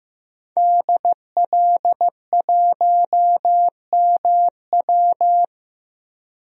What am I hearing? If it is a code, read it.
Morse code, heard as a keyed tone, about 15 wpm, 700 Hz. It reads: DL1MW
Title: DXCC-100 Call Signs - 5 Characters - 15wpm